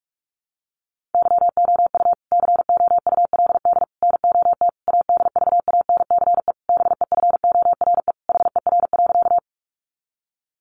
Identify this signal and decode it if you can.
Morse code, heard as a keyed tone, about 44 wpm, 700 Hz. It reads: YOU COULD NOT ADVANCE BEFORE HER.
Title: Sentences from Top 1400 Words 45wpm